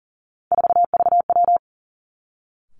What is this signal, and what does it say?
Morse code, heard as a keyed tone, about 40 wpm, 700 Hz. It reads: 4VW
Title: Three Challenging Letter-Number Combinations 40wpm